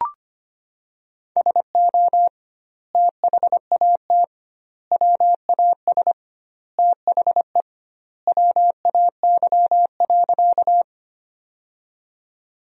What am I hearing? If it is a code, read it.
Morse code, heard as a keyed tone, about 25 wpm, 700 Hz. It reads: SO THAT WAS THE WAY.
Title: 2x Word-Spacing - Sentences from Top 100 Words - 25wpm